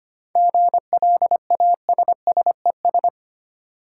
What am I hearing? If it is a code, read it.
Morse code, heard as a keyed tone, about 25 wpm, 700 Hz. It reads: GLASSES
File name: practice / 25wpm / Easy Words with Hints - 25wpm